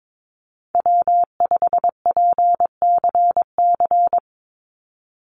Morse code, heard as a keyed tone, about 22 wpm, 700 Hz. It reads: W5PCC
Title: USA Call Signs 22wpm